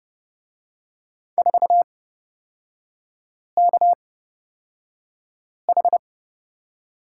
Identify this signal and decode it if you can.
Morse code, heard as a keyed tone, about 30 wpm, spaced as for 5 wpm, 700 Hz. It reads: 4KH
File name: Lesson 36e - BT - Three Characters 30wpm